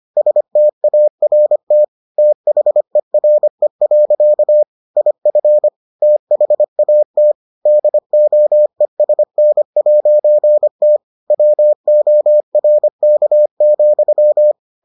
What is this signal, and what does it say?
Morse code, heard as a keyed tone, about 25 wpm, 600 Hz. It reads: START THERE. IF THAT DOESN'T WORK,